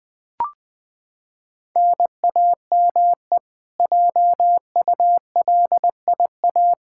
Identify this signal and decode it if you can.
Morse code, heard as a keyed tone, about 20 wpm, 700 Hz. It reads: NAME JULIA
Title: Practice QSO - Exchange 1 - Beginning - 20wpm